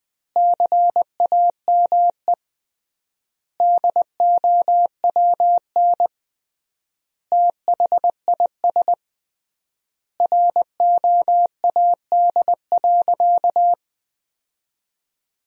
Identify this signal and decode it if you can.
Morse code, heard as a keyed tone, about 20 wpm, 700 Hz. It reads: CAME DOWN THIS ROAD.
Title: 3x Word-Spacing - Sentences from Top 400 Words - 20wpm